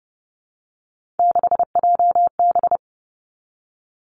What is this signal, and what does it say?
Morse code, heard as a keyed tone, about 30 wpm, 700 Hz. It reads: 6JB
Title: Three Challenging Letter-Number Combinations 30wpm